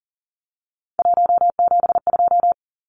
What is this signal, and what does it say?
Morse code, heard as a keyed tone, about 40 wpm, 700 Hz. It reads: 172
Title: Numbers - 3-Digit 40wpm